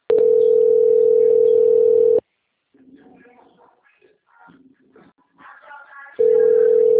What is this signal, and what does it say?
Ringback tone (North America), 440 + 480 Hz.